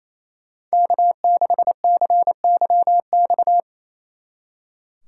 Morse code, heard as a keyed tone, about 35 wpm, 700 Hz. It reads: K6CYX